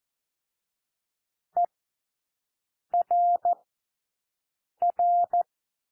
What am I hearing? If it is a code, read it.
Morse code, heard as a keyed tone, about 14 wpm, spaced as for 6 wpm, 700 Hz. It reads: ERR
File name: cwmp3 / W1 SOUND WORDS 14 wpm 0